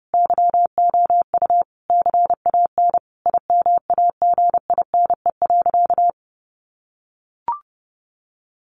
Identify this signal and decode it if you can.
Morse code, heard as a keyed tone, about 30 wpm, 700 Hz. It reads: YOU CAN IMAGINE.
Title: Sentences from Top 1400 Words 30wpm